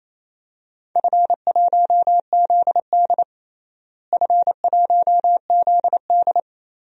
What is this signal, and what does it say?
Morse code, heard as a keyed tone, about 28 wpm, 700 Hz. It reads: F1ZD F1ZD
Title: DXCC-100 Call Signs - 4 Characters - Repeated 2x - 28wpm